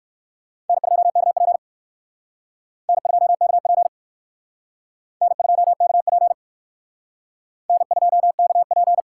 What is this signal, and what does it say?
Morse code, heard as a keyed tone, about 56 wpm, 700 Hz. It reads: N2KP N2KP N2KP N2KP
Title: Speed Racing - USA 1x2 Call Signs 45wpm